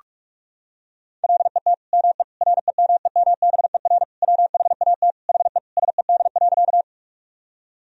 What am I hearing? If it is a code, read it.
Morse code, heard as a keyed tone, about 46 wpm, 700 Hz. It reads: LET ME REMEMBER WHAT HE SED.
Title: Sentences from Top 400 Words and Abbreviations 45wpm